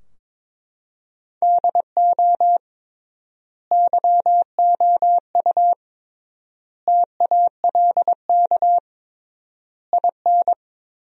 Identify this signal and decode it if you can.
Morse code, heard as a keyed tone, about 21 wpm, 700 Hz. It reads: DO YOU TALK IN
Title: 3x Word-Spacing - Sentences from Top 500 Words - 22wpm